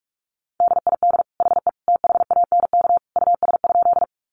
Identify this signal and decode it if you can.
Morse code, heard as a keyed tone, about 44 wpm, 700 Hz. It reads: DID HE THANK US?
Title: Sentences from Top 600 Words 45wpm